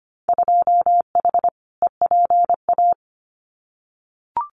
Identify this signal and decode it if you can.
Morse code, heard as a keyed tone, about 25 wpm, 700 Hz. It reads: 2H EPA
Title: Winter Field Day Exchanges 25wpm